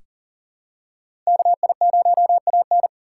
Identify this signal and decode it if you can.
Morse code, heard as a keyed tone, about 40 wpm, 700 Hz. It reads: KI0AN
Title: USA Call Signs - 2x Character Spacing - 40wpm